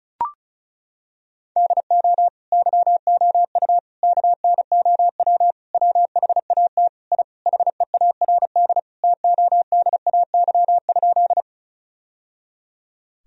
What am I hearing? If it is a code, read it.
Morse code, heard as a keyed tone, about 34 wpm, 700 Hz. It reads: DO YOU KNOW WHAT I HEARD TODAY?
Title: Sentences from Top 1000 Words - Review Part 4 - 35wpm